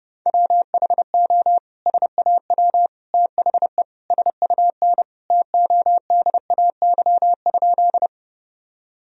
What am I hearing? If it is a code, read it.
Morse code, heard as a keyed tone, about 30 wpm, 700 Hz. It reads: WHO SAW THE SUN TODAY?
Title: Sentences from Top 1000 Words - Review Part 1 - 30wpm